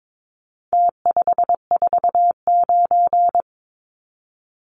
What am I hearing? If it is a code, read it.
Morse code, heard as a keyed tone, about 22 wpm, 700 Hz. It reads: T549
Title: Numbers - Contest Style - 4-Digit 22wpm